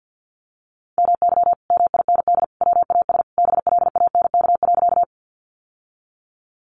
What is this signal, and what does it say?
Morse code, heard as a keyed tone, about 50 wpm, 700 Hz. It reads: MY MIND WAS BLANK.
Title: Sentences from Top 1500 Words 50wpm